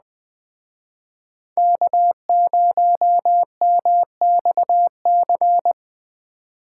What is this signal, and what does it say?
Morse code, heard as a keyed tone, about 20 wpm, 700 Hz. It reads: K0MXC